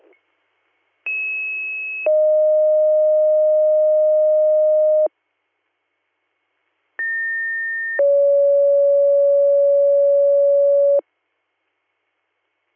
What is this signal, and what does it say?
Two-tone page, heard as a one-second tone, then a three-second tone, about 2600 then 620 Hz, about 1800 then 560 Hz. It is heard from 1.1 to 11.0 s.